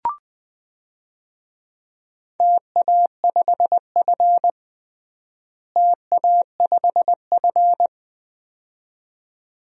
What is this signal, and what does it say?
Morse code, heard as a keyed tone, about 20 wpm, 700 Hz. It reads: TA5F TA5F
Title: DXCC-100 Call Signs - 4 Characters - Repeated 2x - 20wpm